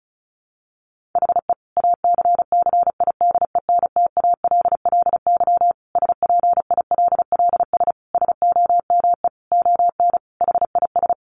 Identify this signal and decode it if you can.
Morse code, heard as a keyed tone, about 34 wpm, 700 Hz. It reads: HE ACCIDENTALLY SPILLS SOME ON HIS
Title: Additional Easy Phrases 35wpm